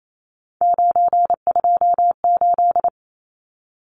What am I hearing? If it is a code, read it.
Morse code, heard as a keyed tone, about 28 wpm, 700 Hz. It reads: 928
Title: Numbers - 3-Digit 28wpm